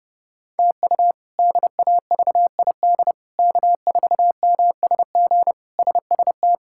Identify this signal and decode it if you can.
Morse code, heard as a keyed tone, about 32 wpm, 700 Hz. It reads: TU DAVID K4MSG SST
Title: K1USN SST QSO Practice - 30wpm